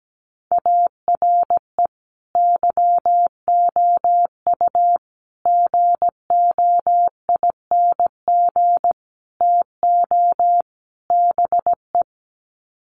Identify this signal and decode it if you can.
Morse code, heard as a keyed tone, about 17 wpm, 700 Hz. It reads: ARE YOU GOING TO BE
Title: Sets of 5 Words Encore 17wpm